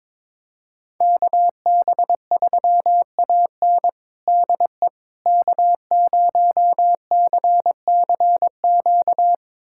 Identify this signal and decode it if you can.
Morse code, heard as a keyed tone, about 22 wpm, 700 Hz. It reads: KB3AN DE K0CCQ
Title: USA Callsign DE Callsign - 5 Characters - 22wpm